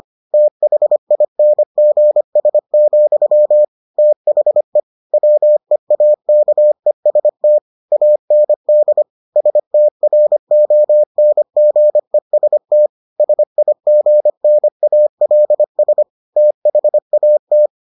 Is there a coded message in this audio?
Yes: Morse code from 0.3 to 17.7 s, heard as a keyed tone, about 25 wpm, 600 Hz. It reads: THINGS, THE WEAKEST AND STRONGEST SIGNALS THAT